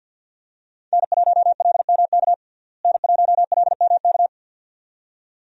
Morse code, heard as a keyed tone, about 49 wpm, 700 Hz. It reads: N1LMK N1LMK
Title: USA Call Signs - 5 Characters - Repeated 2x - 50wpm